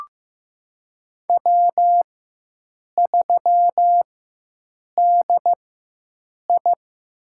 Morse code, heard as a keyed tone, about 15 wpm, spaced as for 7 wpm, 700 Hz. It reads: W3DI
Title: USA Call Signs - 4 Characters - 4x Character Spacing - 15wpm